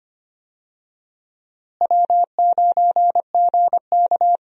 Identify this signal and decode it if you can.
Morse code, heard as a keyed tone, about 25 wpm, 700 Hz. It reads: W9GK
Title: USA Call Signs - 4 Characters - Repeated 3x - 25wpm